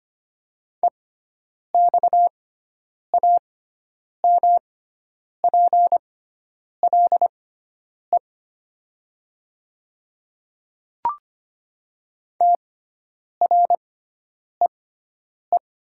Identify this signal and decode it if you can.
Morse code, heard as a keyed tone, about 25 wpm, spaced as for 9 wpm, 700 Hz. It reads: EXAMPLE TREE
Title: 6x Character Spacing - Top 300 Words - 25wpm